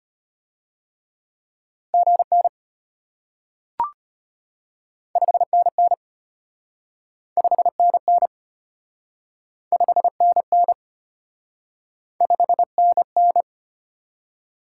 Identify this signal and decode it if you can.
Morse code, heard as a keyed tone, about 33 wpm, 700 Hz. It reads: GN 5NN 5NN 5NN 5NN